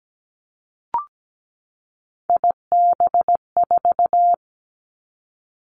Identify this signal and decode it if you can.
Morse code, heard as a keyed tone, about 17 wpm, 700 Hz. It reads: IB4
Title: Three Challenging Letter-Number Combinations 17wpm